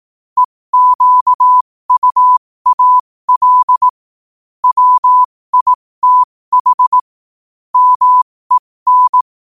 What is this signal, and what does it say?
Morse code, heard as a keyed tone, about 18 wpm, spaced as for 15 wpm, 1000 Hz. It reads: EQUAL WITH MEN
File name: qotdfiles / 15wpm